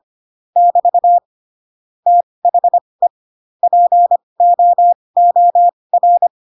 Morse code, heard as a keyed tone, about 25 wpm, spaced as for 20 wpm, 700 Hz. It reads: <BT> THE POOR
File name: Pinocchio / Pinocchio ch7 15 25 ik1hge